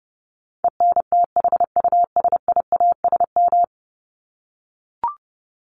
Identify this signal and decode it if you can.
Morse code, heard as a keyed tone, about 30 wpm, 700 Hz. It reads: ENTHUSIASM
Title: Top 1751-2000 Words 30wpm